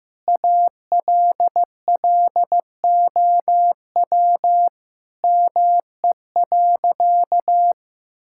Morse code, heard as a keyed tone, about 15 wpm, 700 Hz. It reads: ALLOW ME.
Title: Sentences from Top 1000 Words - Review Part 5 - 15wpm